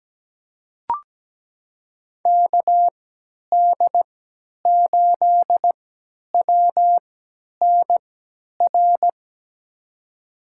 Morse code, heard as a keyed tone, about 17 wpm, spaced as for 10 wpm, 700 Hz. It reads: KD8WNR
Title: USA Call Signs - 6 Characters - 3x Character Spacing - 17wpm